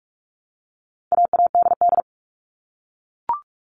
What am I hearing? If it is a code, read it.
Morse code, heard as a keyed tone, about 44 wpm, 700 Hz. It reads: AADD
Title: All Cut Numbers - 4-Digit 45wpm